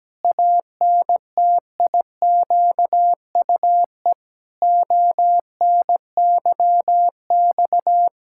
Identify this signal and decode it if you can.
Morse code, heard as a keyed tone, about 17 wpm, 700 Hz. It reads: ANTIQUE ONYX
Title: Pangram Challenge v2 - 17wpm